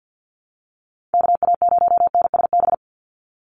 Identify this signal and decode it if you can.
Morse code, heard as a keyed tone, about 50 wpm, 700 Hz. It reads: KA0NSB
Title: USA Call Signs 50wpm